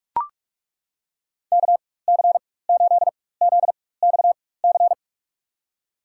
Morse code, heard as a keyed tone, about 42 wpm, spaced as for 20 wpm, 690 Hz. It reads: KC8ZXC
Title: USA Call Signs - 4x Character Spacing - 45wpm